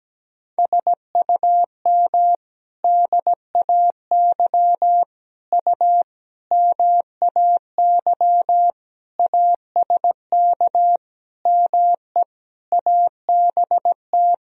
Morse code, heard as a keyed tone, about 17 wpm, 700 Hz. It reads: SUM DAY U MAY ASK ME ABT
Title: Sentences from Top 200 Words and Abbreviations 17wpm